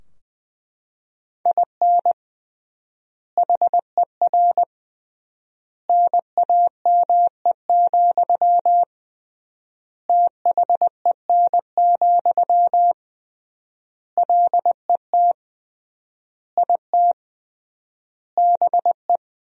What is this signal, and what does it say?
Morse code, heard as a keyed tone, about 21 wpm, 700 Hz. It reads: IN HER NAME, THEN, LET IT BE
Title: 3x Word-Spacing - Sentences from Top 500 Words - 20wpm